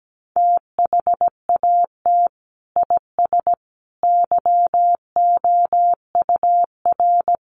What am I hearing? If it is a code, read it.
Morse code, heard as a keyed tone, about 17 wpm, 700 Hz. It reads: THAT IS YOUR
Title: Sentences from Top 1200 Words 17wpm